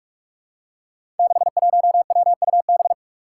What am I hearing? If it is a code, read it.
Morse code, heard as a keyed tone, about 46 wpm, 700 Hz. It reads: 61WUB
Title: Five Challenging Letter-Number Combinations 45wpm